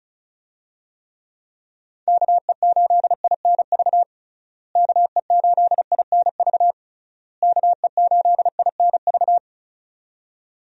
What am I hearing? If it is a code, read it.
Morse code, heard as a keyed tone, about 36 wpm, 700 Hz. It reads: KE8INV KE8INV KE8INV
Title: USA Call Signs - 6 Characters - Repeated 3x - 35wpm